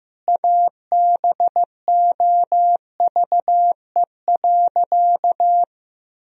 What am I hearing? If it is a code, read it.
Morse code, heard as a keyed tone, about 15 wpm, 700 Hz. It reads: ABOVE.
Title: Sentences from Top 1000 Words - Review Part 4 - 15wpm